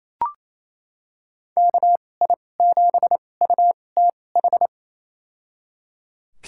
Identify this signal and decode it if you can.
Morse code, heard as a keyed tone, about 28 wpm, spaced as for 20 wpm, 700 Hz. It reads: KI7UTH
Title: USA Call Signs - 6 Characters - 2x Character Spacing - 28wpm